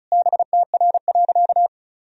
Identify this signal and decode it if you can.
Morse code, heard as a keyed tone, about 34 wpm, 700 Hz. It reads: BTR.
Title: Sentences from Top 400 Words and Abbreviations 35wpm